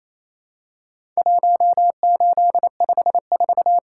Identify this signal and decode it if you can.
Morse code, heard as a keyed tone, about 28 wpm, 700 Hz. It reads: 1854
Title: Worked All Europe DX Contest - QTC Exchange - 28wpm